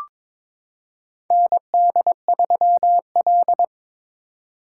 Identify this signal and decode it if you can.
Morse code, heard as a keyed tone, about 22 wpm, 700 Hz. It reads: ND3L
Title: K1USN SST QSO Practice - 22wpm